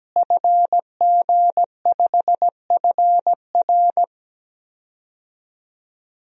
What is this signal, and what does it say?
Morse code, heard as a keyed tone, about 17 wpm, 700 Hz. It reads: FG5FR
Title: DXCC-100 Call Signs - 5 Characters - Repeated 2x - 17wpm